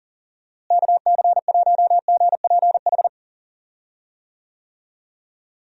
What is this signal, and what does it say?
Morse code, heard as a keyed tone, about 40 wpm, 700 Hz. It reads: KC1GPH
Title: USA Call Signs - 6 Characters - Repeated 3x - 40wpm